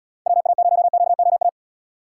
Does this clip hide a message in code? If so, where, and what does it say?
Yes: Morse code from 0.3 to 1.5 s, heard as a keyed tone, about 71 wpm, 700 Hz. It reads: WA0QOA